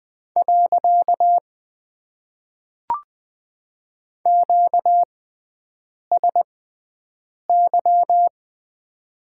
Morse code, heard as a keyed tone, about 20 wpm, spaced as for 7 wpm, 700 Hz. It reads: . QSY
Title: Top 90 QSO Elements - 6x Character Spacing - 20wpm